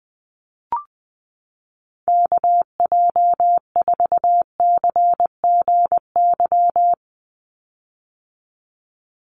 Morse code, heard as a keyed tone, about 20 wpm, 700 Hz. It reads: KJ4CGY
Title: USA Call Signs 20wpm